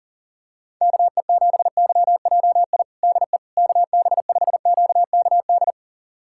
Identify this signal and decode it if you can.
Morse code, heard as a keyed tone, about 40 wpm, 700 Hz. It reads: KE7YJI DE KB5QKD
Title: USA Callsign DE Callsign - 6 Characters - 40wpm